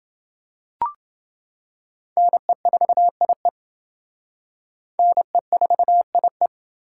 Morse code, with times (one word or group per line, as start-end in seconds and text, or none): 2.2-3.5 NE4IE
5.0-6.5 NE4IE